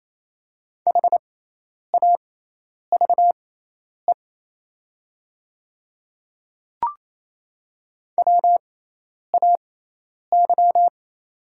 Morse code, heard as a keyed tone, about 27 wpm, spaced as for 10 wpm, 690 Hz. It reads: HAVE WAY
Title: 6x Character Spacing - Top 100 Words - 28wpm